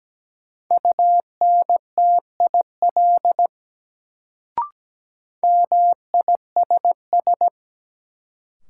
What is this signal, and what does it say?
Morse code, heard as a keyed tone, about 17 wpm, 700 Hz. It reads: UNTIL MISS
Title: Sentences from Top 500 Words - New Word Drill - 17wpm